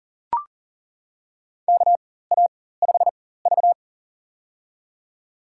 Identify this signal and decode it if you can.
Morse code, heard as a keyed tone, about 39 wpm, spaced as for 18 wpm, 700 Hz. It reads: KA5V
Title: USA Call Signs - 4 Characters - 4x Character Spacing - 40wpm